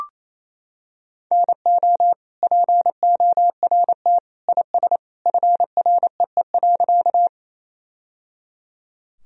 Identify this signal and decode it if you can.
Morse code, heard as a keyed tone, about 28 wpm, 700 Hz. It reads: NO PORT IS FREE.
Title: Sentences from Top 1000 Words - Review Part 1 - 28wpm